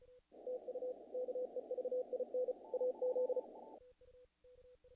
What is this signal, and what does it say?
Morse code, heard as a keyed tone, about 36 wpm, 500 Hz. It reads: TU KEVIN AZ
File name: contests / naqp-cw-k5tr-op-k5pi- -1800